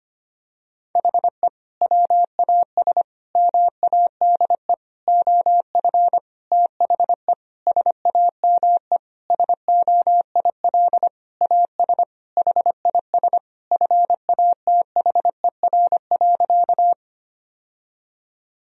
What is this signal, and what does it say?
Morse code, heard as a keyed tone, about 25 wpm, 700 Hz. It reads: HE WAS MADE OF THE SAME SOIL AS HIS FATHER.
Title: Sentences from Top 1000 Words - Review Part 4 - 25wpm